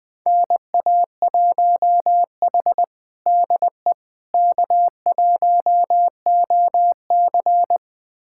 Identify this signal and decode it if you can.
Morse code, heard as a keyed tone, about 20 wpm, 700 Hz. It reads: NA1H DE K1OC
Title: USA Callsign DE Callsign - 4 Characters - 20wpm